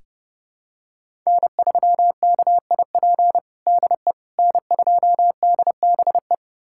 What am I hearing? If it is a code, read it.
Morse code, heard as a keyed tone, about 32 wpm, 700 Hz. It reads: N3KIP DE N2DBE
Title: USA Callsign DE Callsign - 5 Characters - 30wpm